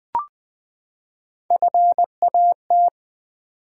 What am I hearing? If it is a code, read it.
Morse code, heard as a keyed tone, about 20 wpm, 700 Hz. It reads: FAT
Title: Three Letter Words - 20wpm